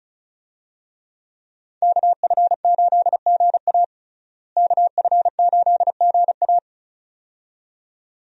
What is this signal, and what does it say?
Morse code, heard as a keyed tone, about 35 wpm, 700 Hz. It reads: KF8GA KF8GA
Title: USA Call Signs - 5 Characters - Repeated 2x - 35wpm